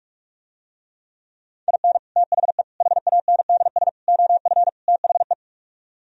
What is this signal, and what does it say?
Morse code, heard as a keyed tone, about 49 wpm, 700 Hz. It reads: IN THE HANDS OF THE